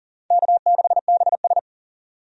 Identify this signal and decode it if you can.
Morse code, heard as a keyed tone, about 37 wpm, 700 Hz. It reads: K6BS
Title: USA Call Signs - 4 Characters - 40wpm